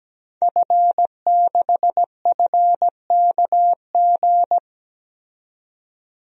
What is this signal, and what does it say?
Morse code, heard as a keyed tone, about 17 wpm, 700 Hz. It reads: F6FKG
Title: DXCC-100 Call Signs - 5 Characters - 17wpm